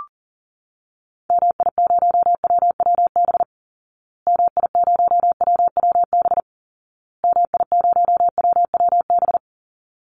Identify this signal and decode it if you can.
Morse code, heard as a keyed tone, about 40 wpm, 700 Hz. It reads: MI0WWB MI0WWB MI0WWB
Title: Call Signs Repeated 2x - 40wpm